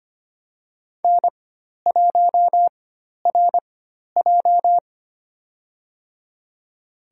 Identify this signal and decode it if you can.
Morse code, heard as a keyed tone, about 25 wpm, spaced as for 12 wpm, 700 Hz. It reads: N1RJ
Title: USA Call Signs - 4 Characters - 4x Character Spacing - 25wpm